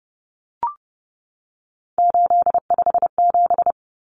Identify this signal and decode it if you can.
Morse code, heard as a keyed tone, about 30 wpm, 700 Hz. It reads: 857
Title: Numbers - 3-Digit 30wpm